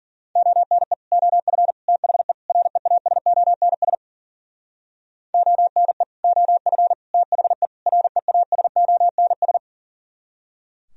Morse code, heard as a keyed tone, about 55 wpm, 700 Hz. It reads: ONE OF THE REASONS ONE OF THE REASONS